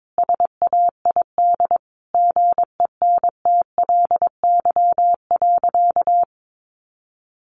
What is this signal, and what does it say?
Morse code, heard as a keyed tone, about 22 wpm, 700 Hz. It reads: SAID GENTLY.
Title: Sentences from Top 1500 Words 22wpm